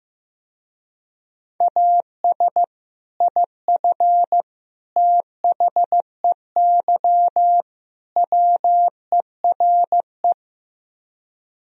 Morse code, heard as a keyed tone, about 15 wpm, 700 Hz. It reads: AS IF THEY WERE
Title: Speed-Racing - Sets of 4 Words 15wpm